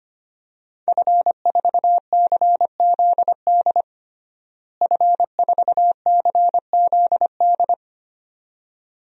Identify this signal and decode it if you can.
Morse code, heard as a keyed tone, about 25 wpm, 700 Hz. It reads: F4CZD F4CZD
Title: DXCC-100 Call Signs - 5 Characters - Repeated 2x - 25wpm